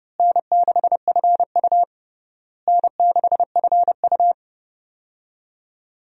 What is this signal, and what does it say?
Morse code, heard as a keyed tone, about 30 wpm, 700 Hz. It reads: N6FU N6FU
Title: USA Call Signs - 4 Characters - Repeated 3x - 30wpm